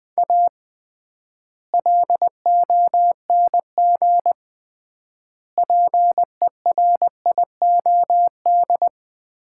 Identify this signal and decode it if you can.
Morse code, heard as a keyed tone, about 20 wpm, 700 Hz. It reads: A LONG PERIOD
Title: 3x Word-Spacing - Sets of 5 Words - 20wpm